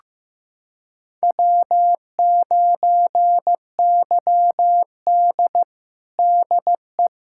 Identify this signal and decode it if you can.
Morse code, heard as a keyed tone, about 15 wpm, 700 Hz. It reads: W9YD DE
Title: USA Callsign DE Callsign - 4 Characters - 15wpm